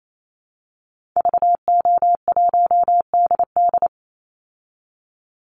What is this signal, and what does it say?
Morse code, heard as a keyed tone, about 28 wpm, 700 Hz. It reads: VO1DD
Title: Call Signs Repeated 2x - 28wpm